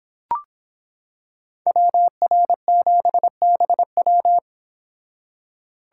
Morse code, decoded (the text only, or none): WR7BW